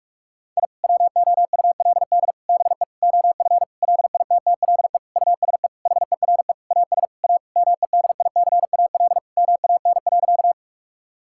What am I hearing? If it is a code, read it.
Morse code, heard as a keyed tone, about 41 wpm, 700 Hz. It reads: I WOULD BE OF LITTLE USE HERE AS A MEDICAL MAN.
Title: Sentences from Top 1000 Words - Review Part 5 - 45wpm